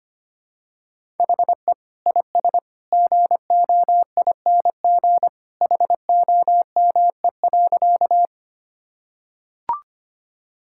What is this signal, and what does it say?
Morse code, heard as a keyed tone, about 25 wpm, 700 Hz. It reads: HE IS GOING HOME.
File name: practice / 25wpm / Sentences with 2-Word Phrases - 25wpm